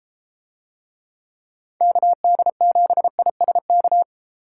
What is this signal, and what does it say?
Morse code, heard as a keyed tone, about 30 wpm, 700 Hz. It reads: KD7ISK